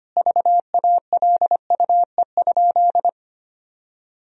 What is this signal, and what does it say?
Morse code, heard as a keyed tone, about 25 wpm, 700 Hz. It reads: VALUE?
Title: Sentences from Top 1000 Words - Review Part 3 - 25wpm